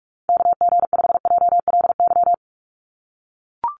Morse code, heard as a keyed tone, about 44 wpm, 700 Hz. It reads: KG5JLY
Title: USA Call Signs 45wpm